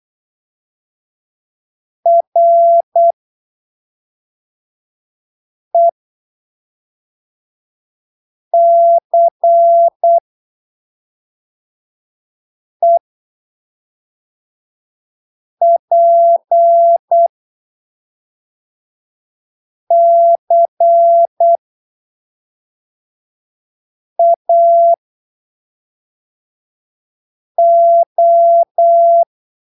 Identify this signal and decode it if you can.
Morse code, heard as a keyed tone, about 8 wpm, spaced as for 3 wpm, 680 Hz. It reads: RECEPCAO
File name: arquivos / texto / texto 18 08ppm